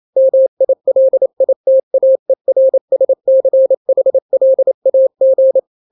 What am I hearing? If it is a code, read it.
Morse code, heard as a keyed tone, about 28 wpm, 530 Hz. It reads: MILITAERSCHLAG
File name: daily / 27wpm